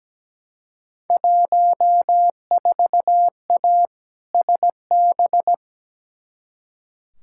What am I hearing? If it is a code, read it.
Morse code, heard as a keyed tone, about 17 wpm, 700 Hz. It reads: 14A SB
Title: Field Day Exchanges Encore - 17wpm